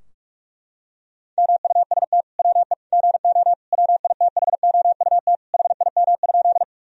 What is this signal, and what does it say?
Morse code, heard as a keyed tone, about 42 wpm, 700 Hz. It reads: MUST WE GO WITHOUT HIM?
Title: Sentences from Top 1000 Words - Review Part 3 - 45wpm